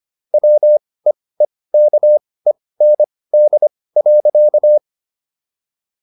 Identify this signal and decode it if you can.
Morse code, heard as a keyed tone, about 25 wpm, spaced as for 18 wpm, 600 Hz. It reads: WEEKEND.